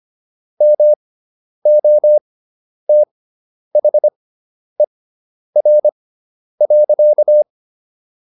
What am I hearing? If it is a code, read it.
Morse code, heard as a keyed tone, about 25 wpm, spaced as for 10 wpm, 600 Hz. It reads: MOTHER.